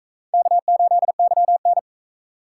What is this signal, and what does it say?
Morse code, heard as a keyed tone, about 40 wpm, 700 Hz. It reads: K8YN